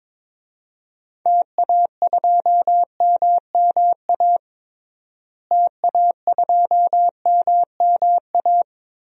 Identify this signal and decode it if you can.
Morse code, heard as a keyed tone, about 22 wpm, 700 Hz. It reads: TA2MMA TA2MMA
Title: DXCC-100 Call Signs - 6 Characters - Repeated 2x - 22wpm